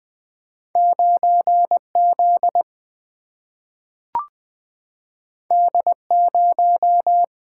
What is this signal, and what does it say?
Morse code, heard as a keyed tone, about 20 wpm, 700 Hz. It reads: 9 Z D 0
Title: Single-Letter Single-Number 20wpm